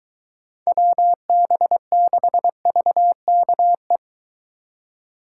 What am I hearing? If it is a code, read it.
Morse code, heard as a keyed tone, about 23 wpm, 700 Hz. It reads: WB6VKE